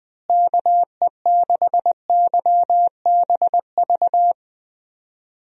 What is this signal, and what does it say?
Morse code, heard as a keyed tone, about 21 wpm, 700 Hz. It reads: KE6YBV
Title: USA Call Signs - 6 Characters - Repeated 3x - 20wpm